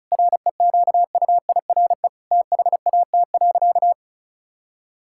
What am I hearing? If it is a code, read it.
Morse code, heard as a keyed tone, about 36 wpm, 700 Hz. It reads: REQUIRE THAT.
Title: Sentences from Top 1000 Words - Review Part 6 - 35wpm